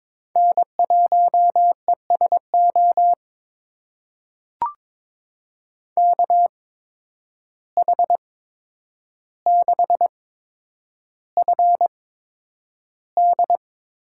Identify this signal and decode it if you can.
Morse code, heard as a keyed tone, about 22 wpm, spaced as for 9 wpm, 700 Hz. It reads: N1ESO KH6FD
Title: USA Call Signs - 5 Characters - 8x Character Spacing - 22wpm